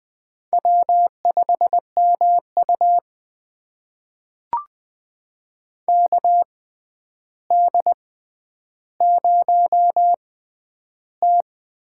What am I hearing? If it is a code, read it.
Morse code, heard as a keyed tone, about 20 wpm, spaced as for 10 wpm, 700 Hz. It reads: W5MU KD0T
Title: USA Call Signs - 6x Character Spacing - 20wpm